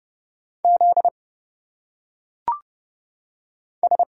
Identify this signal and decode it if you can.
Morse code, heard as a keyed tone, about 30 wpm, 700 Hz. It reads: Z S
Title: Single Letters - Phonetic Pronunciation 30wpm